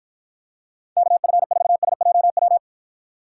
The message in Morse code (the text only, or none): KF4SJW